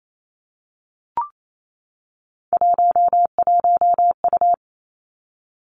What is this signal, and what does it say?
Morse code, heard as a keyed tone, about 28 wpm, 690 Hz. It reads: 11U
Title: Three Challenging Letter-Number Combinations 28wpm